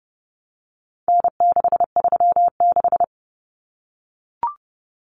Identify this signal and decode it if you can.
Morse code, heard as a keyed tone, about 30 wpm, 700 Hz. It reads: N636
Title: Common Cut Numbers - 4-Digit 30wpm